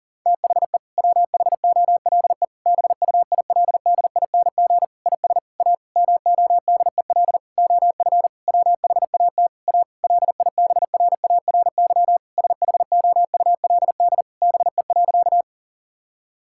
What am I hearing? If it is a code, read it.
Morse code, heard as a keyed tone, about 40 wpm, 700 Hz. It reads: THE WHOLE BUILDING IS A MODEL OF WHAT A LIBRARY SHOULD BE.
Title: Sentences from Top 1000 Words - Review Part 5 - 40wpm